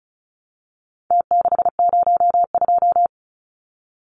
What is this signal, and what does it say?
Morse code, heard as a keyed tone, about 34 wpm, 700 Hz. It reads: T602
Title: Numbers - Contest Style - 4-Digit 35wpm